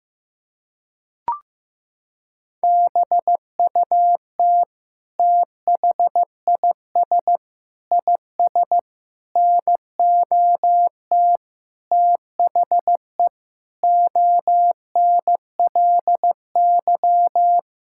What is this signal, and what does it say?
Morse code, heard as a keyed tone, about 16 wpm, 700 Hz. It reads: BUT THIS IS NOT THE ONLY
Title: Sentences from Top 1600 Words 15wpm